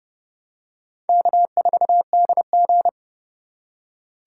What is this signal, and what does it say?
Morse code, heard as a keyed tone, about 30 wpm, 700 Hz. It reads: K4DG
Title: USA Call Signs - 4 Characters - 3x Character Spacing - 30wpm